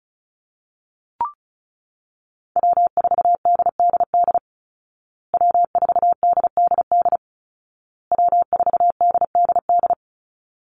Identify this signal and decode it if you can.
Morse code, heard as a keyed tone, about 34 wpm, 700 Hz. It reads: W4DDD W4DDD W4DDD
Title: Call Signs Repeated 2x - 35wpm